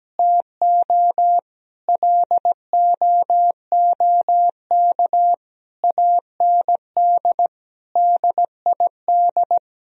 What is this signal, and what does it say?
Morse code, heard as a keyed tone, about 17 wpm, 700 Hz. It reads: TO LOOK AND DID
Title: Sentences from Top 1000 Words - Review Part 2 - 17wpm